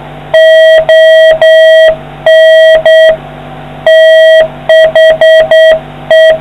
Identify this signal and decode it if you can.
Morse code, heard as a keyed tone, about 8 wpm, 640 Hz. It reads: ON THE